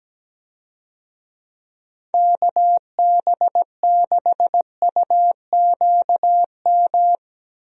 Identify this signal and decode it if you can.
Morse code, heard as a keyed tone, about 17 wpm, 700 Hz. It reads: KB6UQM
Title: USA Call Signs - 6 Characters - Repeated 2x - 17wpm